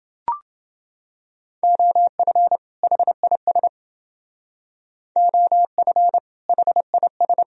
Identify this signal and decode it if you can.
Morse code, heard as a keyed tone, about 29 wpm, 700 Hz. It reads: OF HIS OF HIS